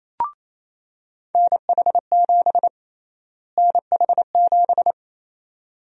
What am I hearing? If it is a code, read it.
Morse code, heard as a keyed tone, about 27 wpm, 700 Hz. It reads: NH7 NH7
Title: USA Call Sign Prefixes - Repeated 2x - 28wpm